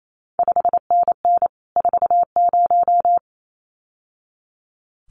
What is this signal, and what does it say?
Morse code, heard as a keyed tone, about 28 wpm, 700 Hz. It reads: 5NN 40
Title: CQ WW DX Contest Exchange Encore - 28wpm